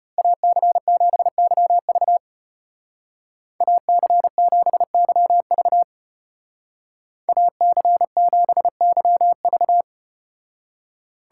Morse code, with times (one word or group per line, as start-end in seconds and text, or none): 0.2-2.2 AC7YV
3.6-5.8 AC7YV
7.3-9.8 AC7YV